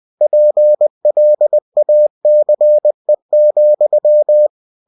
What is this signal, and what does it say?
Morse code, heard as a keyed tone, about 20 wpm, 600 Hz. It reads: PLACE,
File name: cw-books / winnie-the-pooh / winnie-20wpm-page-0082